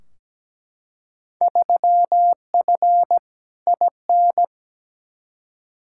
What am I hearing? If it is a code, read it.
Morse code, heard as a keyed tone, about 17 wpm, 700 Hz. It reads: 3F IN
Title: Field Day Exchanges - 17wpm